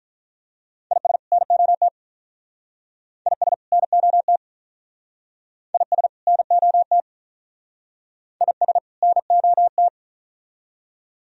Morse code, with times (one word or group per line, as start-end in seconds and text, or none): 0.9-1.2 IS
1.3-1.9 NOT
3.3-3.5 IS
3.7-4.4 NOT
5.7-6.1 IS
6.3-7.0 NOT
8.4-8.8 IS
9.0-9.9 NOT